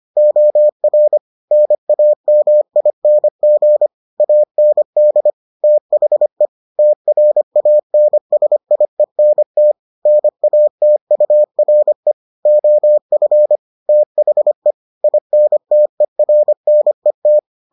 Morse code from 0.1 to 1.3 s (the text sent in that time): OR